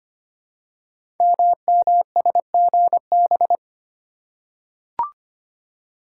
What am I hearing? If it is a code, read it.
Morse code, heard as a keyed tone, about 25 wpm, 700 Hz. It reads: MMSGB
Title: Code Groups - 5 Letters - 25wpm